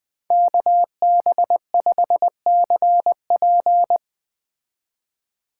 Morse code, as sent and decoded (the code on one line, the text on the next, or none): -.- -... ..... -.-. .--.
KB5CP